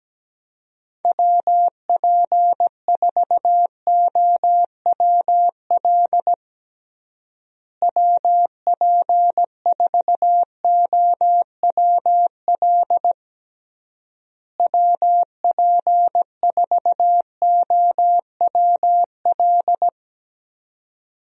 Morse code, heard as a keyed tone, about 17 wpm, 700 Hz. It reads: WP4OWL WP4OWL WP4OWL
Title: USA Call Signs - 6 Characters - Repeated 3x - 17wpm